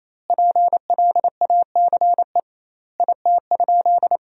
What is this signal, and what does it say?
Morse code, heard as a keyed tone, about 28 wpm, 700 Hz. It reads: PLACE IT?
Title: 2x Word-Spacing - Sentences from Top 200 Words - 28wpm